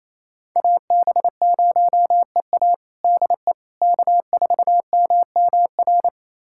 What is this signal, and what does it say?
Morse code, heard as a keyed tone, about 28 wpm, 700 Hz. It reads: AB0EA DE K4MMR
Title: USA Callsign DE Callsign - 5 Characters - 28wpm